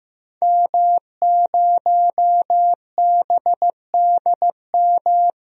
Morse code, heard as a keyed tone, about 15 wpm, 700 Hz. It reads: M0BDM
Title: DXCC-100 Call Signs - 5 Characters - Repeated 3x - 15wpm